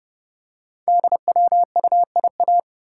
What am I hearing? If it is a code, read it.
Morse code, heard as a keyed tone, about 30 wpm, 700 Hz. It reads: DWUIA
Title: Code Groups - 5 Letter:Numbers - 30wpm